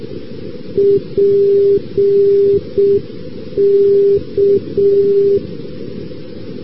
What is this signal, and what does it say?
Morse code, heard as a keyed tone, about 6 wpm, 400 Hz. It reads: PK
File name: europe / rus / rus-2189 ndb st. petersburg-pulkovo pk-342-np-fin-maakeskidx- a-rb